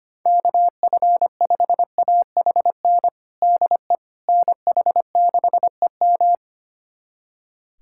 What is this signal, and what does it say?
Morse code, heard as a keyed tone, about 25 wpm, 700 Hz. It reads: KF5AHN DE NH6EM
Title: USA Callsign DE Callsign 25wpm